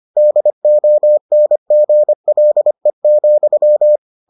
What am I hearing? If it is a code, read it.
Morse code, heard as a keyed tone, about 25 wpm, 600 Hz. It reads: DONGLE,